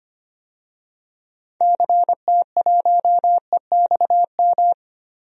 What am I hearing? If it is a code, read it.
Morse code, heard as a keyed tone, about 25 wpm, 700 Hz. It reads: CT1EXM
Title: DXCC-100 Call Signs - 6 Characters - Repeated 2x - 25wpm